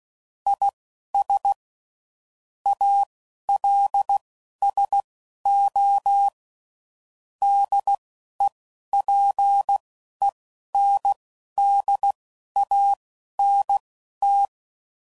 Morse code, heard as a keyed tone, about 16 wpm, spaced as for 11 wpm, 800 Hz. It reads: IS ALSO DEPENDANT
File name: CW Code Groups @10WPM / Technical Subjects / XC